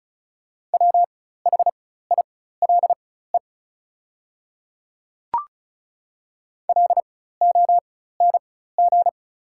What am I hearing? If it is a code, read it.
Morse code, heard as a keyed tone, about 36 wpm, spaced as for 16 wpm, 700 Hz. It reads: WHILE LONG
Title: 4x Character Spacing - Top 300 Words - 35wpm